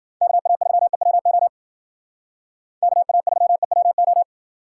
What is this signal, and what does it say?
Morse code, heard as a keyed tone, about 52 wpm, 700 Hz. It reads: KA3EWO KA3EWO